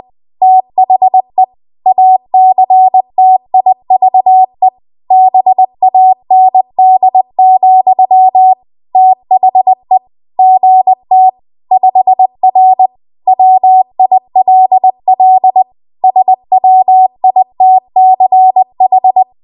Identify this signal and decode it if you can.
Morse code, heard as a keyed tone, about 20 wpm, 750 Hz. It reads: THE ACTIVE BAND, THE GT 5R WILL SWITCH
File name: Archive / 20 WPM / 20WPM